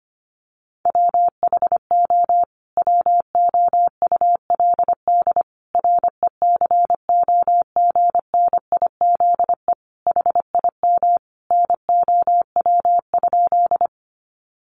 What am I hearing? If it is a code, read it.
Morse code, heard as a keyed tone, about 25 wpm, 700 Hz. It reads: WHO WOULD RECOGNIZE HIM NOW?
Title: Sentences from Top 1400 Words 25wpm